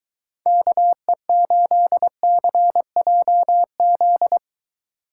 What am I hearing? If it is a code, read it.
Morse code, heard as a keyed tone, about 23 wpm, 700 Hz. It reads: KE8CJZ